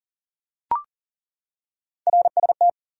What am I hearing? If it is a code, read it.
Morse code, heard as a keyed tone, about 39 wpm, 700 Hz. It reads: RST
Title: Top 180 QSO Elements 40wpm